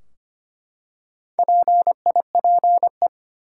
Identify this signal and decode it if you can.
Morse code, heard as a keyed tone, about 25 wpm, 700 Hz. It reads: PIPE
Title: Four Letter Words - 25wpm